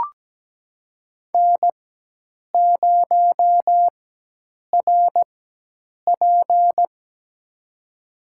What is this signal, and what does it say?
Morse code, heard as a keyed tone, about 17 wpm, spaced as for 8 wpm, 700 Hz. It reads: N0RP
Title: USA Call Signs - 4 Characters - 4x Character Spacing - 17wpm